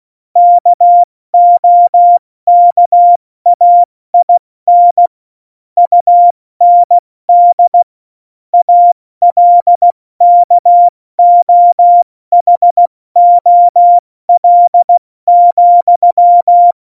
Morse code, heard as a keyed tone, about 16 wpm, 700 Hz. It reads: KOKAIN UND ALKOHOL,
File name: Morsekurs / Witze / Witz 01. Dezember 2020, 70 F80